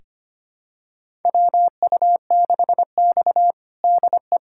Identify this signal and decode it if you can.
Morse code, heard as a keyed tone, about 25 wpm, 700 Hz. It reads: WU6X DE
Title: USA Callsign DE Callsign 25wpm